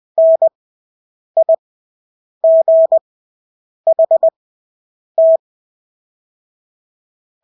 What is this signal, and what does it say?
Morse code, heard as a keyed tone, about 20 wpm, spaced as for 8 wpm, 650 Hz. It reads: NIGHT
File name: Engelske ord 20/8 wpm